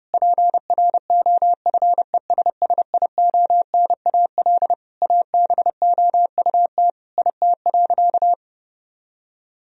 Morse code, heard as a keyed tone, about 30 wpm, 700 Hz. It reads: PROFESSIONAL ABOUT IT.
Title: Sentences from Top 1000 Words - Review Part 4 - 30wpm